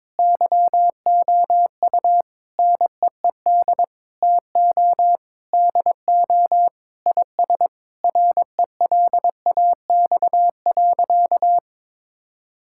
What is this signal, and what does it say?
Morse code, heard as a keyed tone, about 22 wpm, 700 Hz. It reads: YOU NEED TO DO IS RELAX.
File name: practice / 22wpm / Sentences with 8-Word Phrases - 22wpm